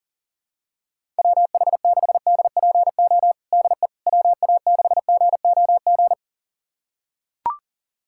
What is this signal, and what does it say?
Morse code, heard as a keyed tone, about 40 wpm, 700 Hz. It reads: WH6DPO DE WA6GOG
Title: USA Callsign DE Callsign 40wpm